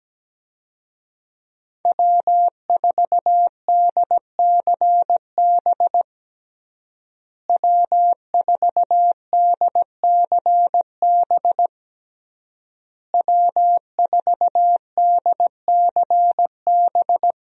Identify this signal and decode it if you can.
Morse code, heard as a keyed tone, about 17 wpm, 700 Hz. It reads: W4DCB W4DCB W4DCB
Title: USA Call Signs - 5 Characters - Repeated 3x - 17wpm